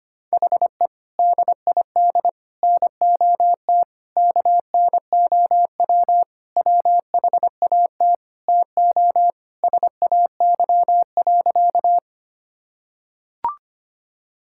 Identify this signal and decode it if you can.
Morse code, heard as a keyed tone, about 25 wpm, 700 Hz. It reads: HE DID NOT KNOW WHAT TO SAY.
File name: practice / 25wpm / Sentences with 3-Word Phrases - 25wpm